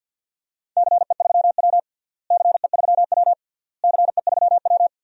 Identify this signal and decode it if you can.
Morse code, heard as a keyed tone, about 49 wpm, 700 Hz. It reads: CE3W CE3W CE3W
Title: DXCC-100 Call Signs - 4 Characters - Repeated 3x - 50wpm